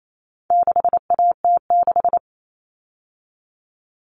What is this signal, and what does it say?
Morse code, heard as a keyed tone, about 28 wpm, 700 Hz. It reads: 6AT6
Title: All Cut Numbers - 4-Digit 28wpm